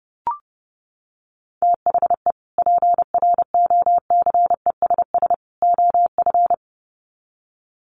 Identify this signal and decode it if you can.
Morse code, heard as a keyed tone, about 30 wpm, 700 Hz. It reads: THE PROCESS OF
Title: Sets of 3 Words 30wpm